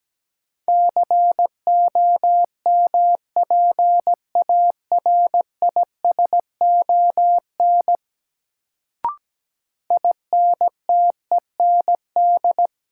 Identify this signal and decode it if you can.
Morse code, heard as a keyed tone, about 17 wpm, 700 Hz. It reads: COMPARISON INTEND
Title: Sentences from Top 1600 Words - New Word Drill - 17wpm